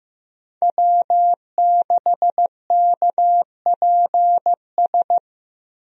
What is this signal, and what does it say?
Morse code, heard as a keyed tone, about 15 wpm, 700 Hz. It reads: W6KPS
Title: USA Call Signs - 5 Characters - Repeated 2x - 15wpm